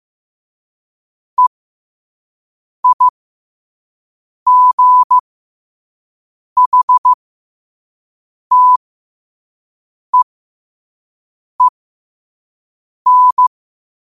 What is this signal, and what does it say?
Morse code, heard as a keyed tone, about 15 wpm, spaced as for 5 wpm, 1000 Hz. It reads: EIGHTEEN
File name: qotdfiles / 5wpm